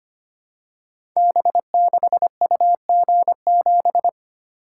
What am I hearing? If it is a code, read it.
Morse code, heard as a keyed tone, about 25 wpm, 700 Hz. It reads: B6UG7
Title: Code Groups - 5 Letter:Numbers - 25wpm